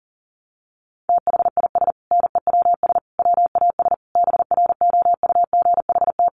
Morse code, heard as a keyed tone, about 37 wpm, 700 Hz. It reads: THIS NEWS WAS BROUGHT
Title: Sentences from Top 800 Words 40wpm